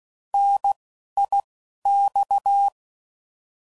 Morse code, heard as a keyed tone, about 16 wpm, spaced as for 12 wpm, 800 Hz. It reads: NIX
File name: Mixed / Contrasts / 3LtrContrasts